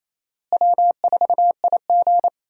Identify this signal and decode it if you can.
Morse code, heard as a keyed tone, about 28 wpm, 700 Hz. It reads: W4IG
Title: USA Call Signs - 4 Characters - Repeated 3x - 28wpm